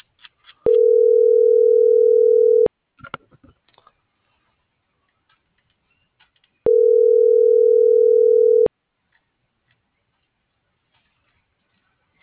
Ringback tone (North America), 440 + 480 Hz.